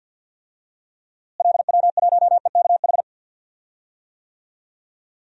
Morse code, heard as a keyed tone, about 53 wpm, 700 Hz. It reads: LW1EKH
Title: DXCC-100 Call Signs - 6 Characters - Repeated 2x - 50wpm